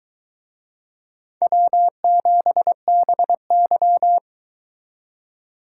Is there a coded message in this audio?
Yes: Morse code from 1.4 to 4.2 s, heard as a keyed tone, about 21 wpm, 700 Hz. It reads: W7BY